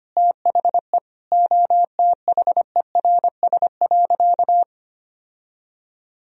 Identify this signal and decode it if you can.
Morse code, heard as a keyed tone, about 25 wpm, 700 Hz. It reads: THE OTHERS.
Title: Sentences from Top 1000 Words - Review Part 2 - 25wpm